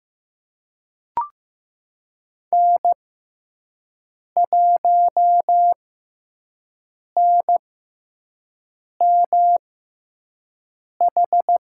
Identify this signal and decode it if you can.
Morse code, heard as a keyed tone, about 15 wpm, spaced as for 5 wpm, 700 Hz. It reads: N1NMH
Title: USA Call Signs - 5 Characters - 6x Character Spacing - 15wpm